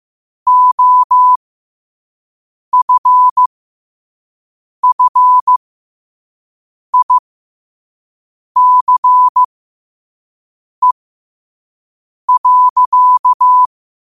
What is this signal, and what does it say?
Morse code, heard as a keyed tone, about 15 wpm, spaced as for 5 wpm, 1000 Hz. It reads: OFFICE.
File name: qotdfiles / 5wpm